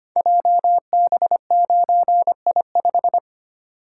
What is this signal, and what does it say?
Morse code, heard as a keyed tone, about 25 wpm, 700 Hz. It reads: JB9I5
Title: Code Groups - 5 Letter:Numbers - 25wpm